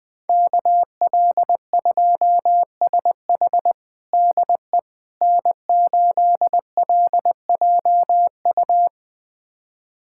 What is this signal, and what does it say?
Morse code, heard as a keyed tone, about 21 wpm, 700 Hz. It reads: KL2SH DE N8LJU
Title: USA Callsign DE Callsign 20wpm